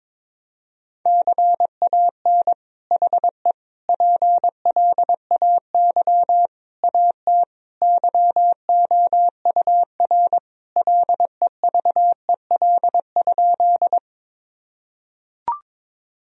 Morse code, heard as a keyed tone, about 22 wpm, 700 Hz. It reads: CAN HE PLAY AT YOUR LEVEL?
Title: Sentences from Top 1000 Words - Review Part 4 - 22wpm